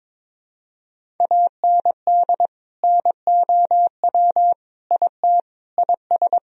Morse code, heard as a keyed tone, about 22 wpm, 700 Hz. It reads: AND NOW IT IS
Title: Sentences from Top 1000 Words - Review Part 1 - 22wpm